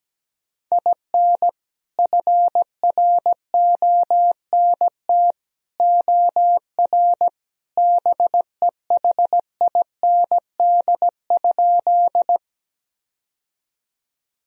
Morse code, heard as a keyed tone, about 17 wpm, 700 Hz. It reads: IN FRONT OR BEHIND?
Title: Sentences from Top 1000 Words - Review Part 2 - 17wpm